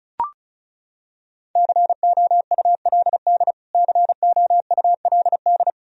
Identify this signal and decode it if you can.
Morse code, heard as a keyed tone, about 34 wpm, 700 Hz. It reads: COULD COULD
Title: Top 100 Words Repeated 35wpm